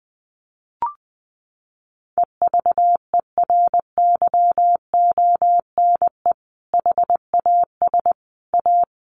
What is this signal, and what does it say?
Morse code, heard as a keyed tone, about 20 wpm, 700 Hz. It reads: EVERYONE HAS A
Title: Sentences from Top 1200 Words 20wpm